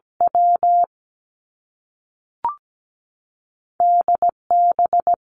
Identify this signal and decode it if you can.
Morse code, heard as a keyed tone, about 17 wpm, 700 Hz. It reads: W D B
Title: Two Challenging Letter-Number Combinations 17wpm